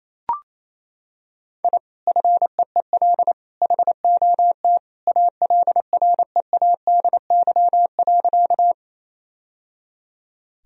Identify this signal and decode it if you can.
Morse code, heard as a keyed tone, about 27 wpm, 700 Hz. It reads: I FEEL HOT ALREADY.
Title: Sentences from Top 1000 Words - Review Part 3 - 28wpm